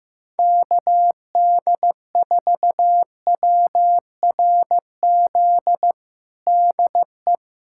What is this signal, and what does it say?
Morse code, heard as a keyed tone, about 15 wpm, 700 Hz. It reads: KD4WRZ DE
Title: USA Callsign DE Callsign - 6 Characters - 15wpm